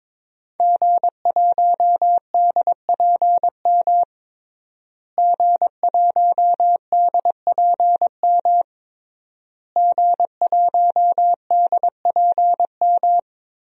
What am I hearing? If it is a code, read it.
Morse code, heard as a keyed tone, about 22 wpm, 700 Hz. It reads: G1DPM G1DPM G1DPM
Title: DXCC-100 Call Signs - 5 Characters - Repeated 3x - 22wpm